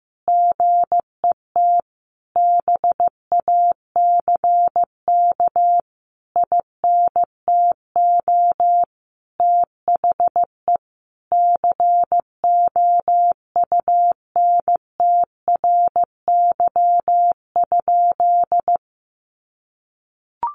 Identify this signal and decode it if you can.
Morse code, heard as a keyed tone, about 15 wpm, 700 Hz. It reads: GET BACK INTO THE COUNTRY?
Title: Sentences from Top 1400 Words 15wpm